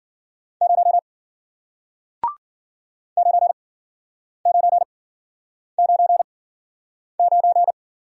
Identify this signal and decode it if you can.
Morse code, heard as a keyed tone, about 52 wpm, spaced as for 9 wpm, 700 Hz. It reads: 0 9999